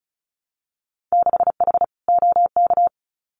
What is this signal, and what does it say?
Morse code, heard as a keyed tone, about 36 wpm, 700 Hz. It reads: 6H OK
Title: Winter Field Day Exchanges Encore 35wpm